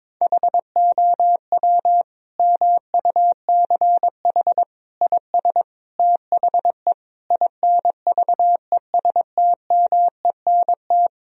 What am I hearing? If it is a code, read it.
Morse code, heard as a keyed tone, about 22 wpm, 700 Hz. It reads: HOW MUCH IS THE INVESTMENT
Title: Sentences from Top 1000 Words - Review Part 4 - 22wpm